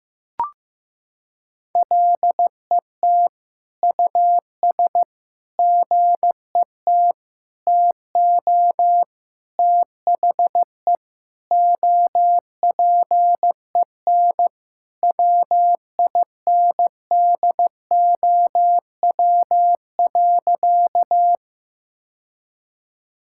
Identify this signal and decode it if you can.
Morse code, heard as a keyed tone, about 15 wpm, 700 Hz. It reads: LET US GET TO THE OPEN WINDOW.
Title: Sentences from Top 1600 Words 15wpm